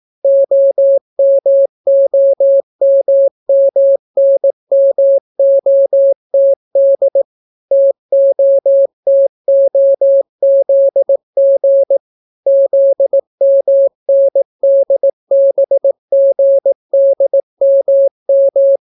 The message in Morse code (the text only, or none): OMOMMNMOTD TOTOZG ZMNDBGDMM